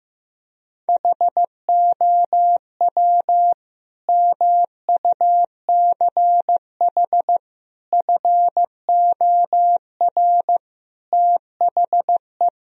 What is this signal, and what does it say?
Morse code, heard as a keyed tone, about 15 wpm, 700 Hz. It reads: HOW MUCH FOR THE
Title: Sentences from Top 1000 Words - Review Part 3 - 15wpm